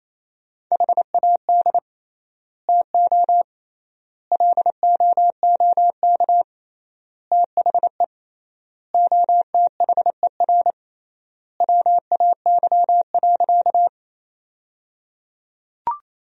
Morse code, heard as a keyed tone, about 27 wpm, 700 Hz. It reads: HAD TO LOOK THE OTHER WAY.
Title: 3x Word-Spacing - Sentences from Top 100 Words - 28wpm